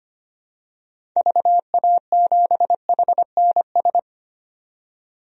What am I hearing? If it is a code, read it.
Morse code, heard as a keyed tone, about 25 wpm, 700 Hz. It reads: VA7HNS
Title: DXCC-100 Call Signs - 6 Characters - 25wpm